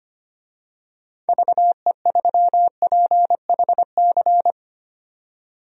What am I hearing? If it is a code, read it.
Morse code, heard as a keyed tone, about 25 wpm, 700 Hz. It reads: VE3PHC
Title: DXCC-100 Call Signs - 6 Characters - 25wpm